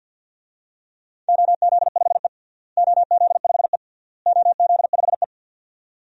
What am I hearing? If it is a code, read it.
Morse code, heard as a keyed tone, about 51 wpm, 700 Hz. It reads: OZ5E OZ5E OZ5E
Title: DXCC-100 Call Signs - 4 Characters - Repeated 3x - 50wpm